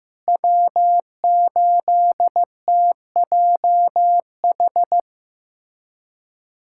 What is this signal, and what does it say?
Morse code, heard as a keyed tone, about 15 wpm, 700 Hz. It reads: W8TJH
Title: USA Callsign DE Callsign - 5 Characters - 15wpm